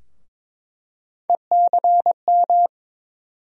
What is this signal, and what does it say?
Morse code, heard as a keyed tone, about 22 wpm, 700 Hz. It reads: ECM
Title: USA Call Sign Postfixes - 3 Letters 22wpm